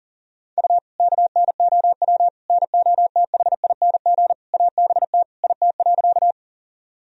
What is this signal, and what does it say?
Morse code, heard as a keyed tone, about 40 wpm, 700 Hz. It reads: U KNOW NOTHING ABT IT.
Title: Sentences from Top 500 Words and Abbreviations 40wpm